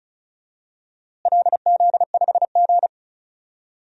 Morse code, heard as a keyed tone, about 36 wpm, 700 Hz. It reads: LZ5G
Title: DXCC-100 Call Signs - 4 Characters - Repeated 3x - 35wpm